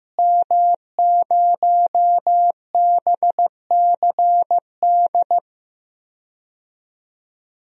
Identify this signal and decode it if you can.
Morse code, heard as a keyed tone, about 15 wpm, 700 Hz. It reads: M0BCD
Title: DXCC-100 Call Signs - 5 Characters - Repeated 3x - 15wpm